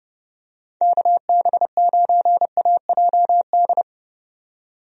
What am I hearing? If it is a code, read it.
Morse code, heard as a keyed tone, about 30 wpm, 700 Hz. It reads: KB9AJD
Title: USA Call Signs - 6 Characters - 30wpm